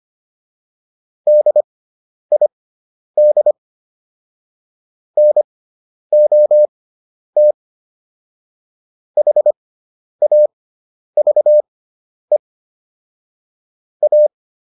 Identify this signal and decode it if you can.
Morse code, heard as a keyed tone, about 25 wpm, spaced as for 10 wpm, 600 Hz. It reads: DID NOT HAVE A
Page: intermediate-practice-files